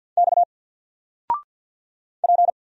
Morse code, heard as a keyed tone, about 50 wpm, 700 Hz. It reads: X P
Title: Single Letter-Number 50wpm